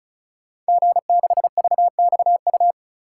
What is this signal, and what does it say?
Morse code, heard as a keyed tone, about 36 wpm, 700 Hz. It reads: G6VXU
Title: Code Groups - 5 Letter:Numbers - 35wpm